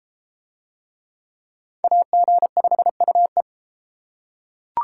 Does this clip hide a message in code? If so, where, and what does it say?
Yes: Morse code from 1.8 to 3.4 s, heard as a keyed tone, about 31 wpm, 700 Hz. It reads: AG5UE